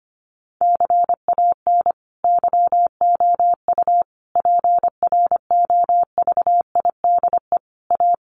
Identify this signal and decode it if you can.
Morse code, heard as a keyed tone, about 25 wpm, 700 Hz. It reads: CAN YOU PROVIDE A
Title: Sentences from Top 1500 Words 25wpm